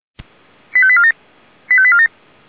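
Five-tone selective call (ZVEI), heard as quick runs of five tones, tones of about 70 ms. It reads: 85847 85847
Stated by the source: ZVEI five-tone selective call received by SDR.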